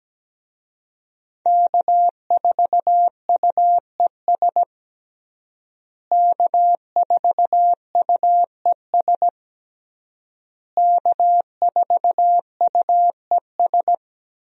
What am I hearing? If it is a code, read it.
Morse code, heard as a keyed tone, about 17 wpm, 700 Hz. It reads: K4UES K4UES K4UES
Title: USA Call Signs - 5 Characters - Repeated 3x - 17wpm